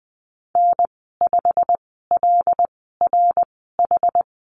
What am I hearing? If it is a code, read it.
Morse code, heard as a keyed tone, about 20 wpm, spaced as for 14 wpm, 700 Hz. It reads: N5LRH
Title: 088 Lesson 15g - Number 5 - Call Signs 20wpm